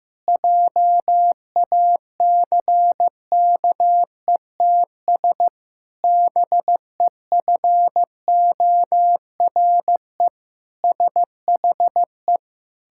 Morse code, heard as a keyed tone, about 14 wpm, 700 Hz. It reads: JACKETS BEFORE SHE
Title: Pangram Challenge v2 - 15wpm